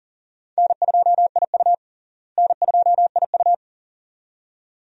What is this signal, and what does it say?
Morse code, heard as a keyed tone, about 40 wpm, 700 Hz. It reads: N2IU N2IU
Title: USA Call Signs - 4 Characters - Repeated 2x - 40wpm